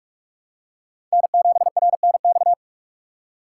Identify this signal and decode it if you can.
Morse code, heard as a keyed tone, about 42 wpm, 700 Hz. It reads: N7RNX
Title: USA Call Signs - 6x Character Spacing - 45wpm